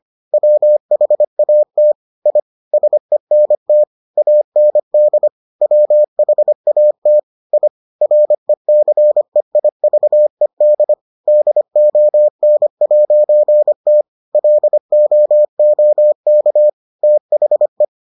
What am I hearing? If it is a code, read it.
Morse code, heard as a keyed tone, about 25 wpm, 600 Hz. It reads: WHAT I SENT AND WHAT I RECEIVED DON'T LOOK THE